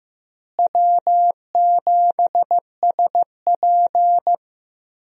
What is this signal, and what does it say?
Morse code, heard as a keyed tone, about 15 wpm, 700 Hz. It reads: W7SP
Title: USA Call Signs - 4 Characters - Repeated 3x - 15wpm